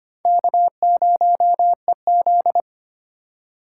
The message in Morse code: K0EZ

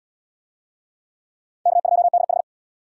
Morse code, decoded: K2DH

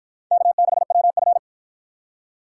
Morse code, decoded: K6WF

